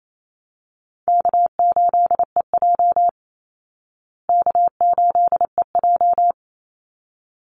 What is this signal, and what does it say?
Morse code, heard as a keyed tone, about 28 wpm, 700 Hz. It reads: K8EJ K8EJ
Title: Call Signs Repeated 3x - 28wpm